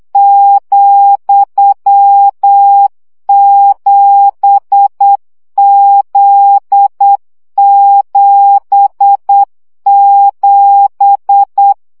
Morse code, heard as a keyed tone, about 8 wpm, 800 Hz. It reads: ,7Z77
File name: CW gyakorlas z 7 vesszo 7 baud